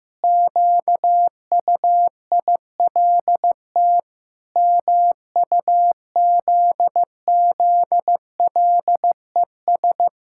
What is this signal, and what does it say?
Morse code, heard as a keyed tone, about 15 wpm, 700 Hz. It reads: QUILT MUZZLES
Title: Pangram Challenge v2 - 15wpm